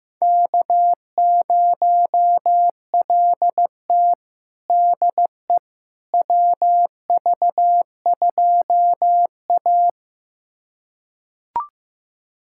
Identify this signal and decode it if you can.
Morse code, heard as a keyed tone, about 15 wpm, 700 Hz. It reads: K0LT DE WV2A
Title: USA Callsign DE Callsign - 4 Characters - 15wpm